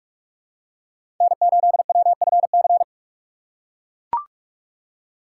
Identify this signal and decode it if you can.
Morse code, heard as a keyed tone, about 43 wpm, 700 Hz. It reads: N8WFC
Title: USA Call Signs - 2x Character Spacing - 45wpm